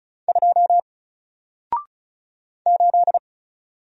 Morse code, heard as a keyed tone, about 35 wpm, 700 Hz. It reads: 2 8
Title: Single Character (Letter:Number:Punctuation:Prosign) 35wpm